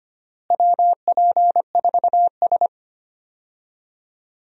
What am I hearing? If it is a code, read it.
Morse code, heard as a keyed tone, about 25 wpm, 700 Hz. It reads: WP4S
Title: USA Call Signs - 4 Characters - Repeated 3x - 25wpm